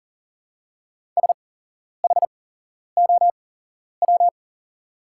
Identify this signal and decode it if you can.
Morse code, heard as a keyed tone, about 40 wpm, spaced as for 11 wpm, 700 Hz. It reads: SHOW
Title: 8x Character Spacing - Top 300 Words - 40wpm